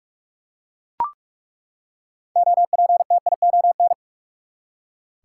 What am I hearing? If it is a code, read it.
Morse code, heard as a keyed tone, about 42 wpm, 700 Hz. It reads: OPTION
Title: Six Letter Words - 45wpm